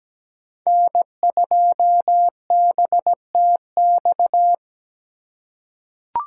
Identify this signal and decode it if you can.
Morse code, heard as a keyed tone, about 17 wpm, 700 Hz. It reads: N2BTX
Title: K1USN SST QSO Practice - 17wpm